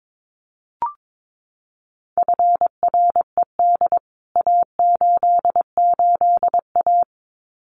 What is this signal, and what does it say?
Morse code, heard as a keyed tone, about 22 wpm, 700 Hz. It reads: FRED A88A
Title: CWT Exchange 22wpm